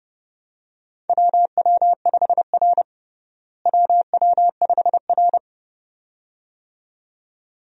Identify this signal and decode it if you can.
Morse code, heard as a keyed tone, about 30 wpm, 700 Hz. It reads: WW5R WW5R
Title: USA Call Signs - 4 Characters - Repeated 2x - 30wpm